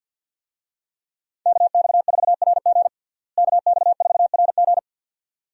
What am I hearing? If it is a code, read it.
Morse code, heard as a keyed tone, about 52 wpm, 700 Hz. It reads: KX4RG KX4RG
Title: USA Call Signs - 5 Characters - Repeated 2x - 50wpm